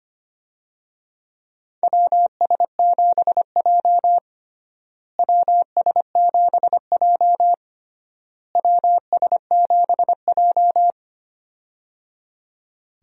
Morse code, heard as a keyed tone, about 25 wpm, 700 Hz. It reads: WS7J WS7J WS7J
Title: USA Call Signs - 4 Characters - Repeated 3x - 25wpm